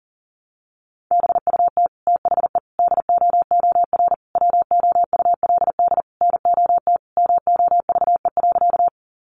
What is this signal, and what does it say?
Morse code, heard as a keyed tone, about 40 wpm, 700 Hz. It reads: BUT THE DOOR WOULD NOT MOVE.
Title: Sentences from Top 600 Words 40wpm